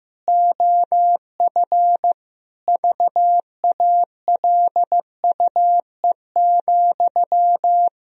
Morse code, heard as a keyed tone, about 15 wpm, 700 Hz. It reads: OF VALUE,
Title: Sentences from Top 1000 Words - Review Part 3 - 15wpm